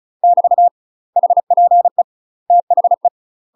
Morse code, heard as a keyed tone, about 34 wpm, 700 Hz. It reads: <BT> HPE THE
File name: UK MASTER QSOs TRAINING / 35WPM / UK MASTER QSOs 0039